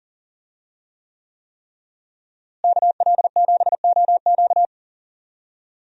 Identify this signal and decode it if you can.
Morse code, heard as a keyed tone, about 40 wpm, 700 Hz. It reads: KL7OQ
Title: DXCC-100 Call Signs - 5 Characters - Repeated 3x - 40wpm